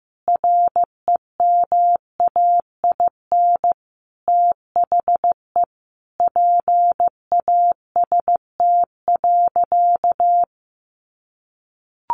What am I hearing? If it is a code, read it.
Morse code, heard as a keyed tone, about 15 wpm, 700 Hz. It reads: REMAIN THE PAST.
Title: Sentences from Top 1200 Words 15wpm